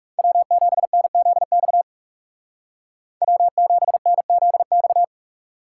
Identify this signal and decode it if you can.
Morse code, heard as a keyed tone, about 37 wpm, 700 Hz. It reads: W7NZX W7NZX
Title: Speed Racing - USA Call Signs 30wpm